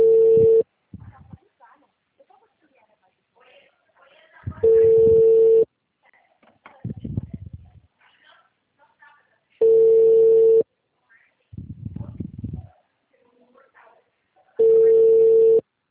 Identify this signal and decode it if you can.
Ringback tone (China, 450 Hz), with speech.
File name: out- -2010